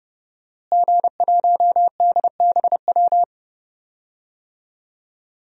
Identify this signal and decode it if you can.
Morse code, heard as a keyed tone, about 32 wpm, 700 Hz. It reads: G1DBW
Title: DXCC-100 Call Signs - 5 Characters - Repeated 3x - 30wpm